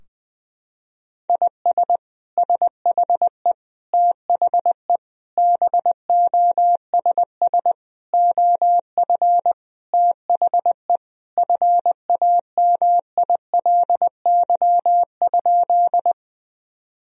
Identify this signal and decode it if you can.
Morse code, heard as a keyed tone, about 21 wpm, 700 Hz. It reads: IS SHE THE BOSS OF THE FAMILY?
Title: Sentences from Top 1000 Words - Review Part 4 - 20wpm